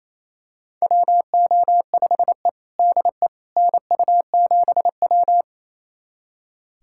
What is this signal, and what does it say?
Morse code, heard as a keyed tone, about 28 wpm, 700 Hz. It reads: WO5E DE NU7W
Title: USA Callsign DE Callsign - 4 Characters - 28wpm